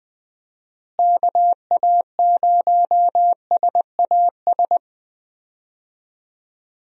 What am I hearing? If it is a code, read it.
Morse code, heard as a keyed tone, about 20 wpm, 700 Hz. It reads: KA0SAS
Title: USA Call Signs - 6 Characters - Repeated 3x - 20wpm